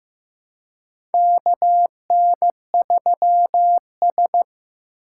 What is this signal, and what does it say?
Morse code, heard as a keyed tone, about 15 wpm, 700 Hz. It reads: KN3S
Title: USA Call Signs - 4 Characters - 4x Character Spacing - 15wpm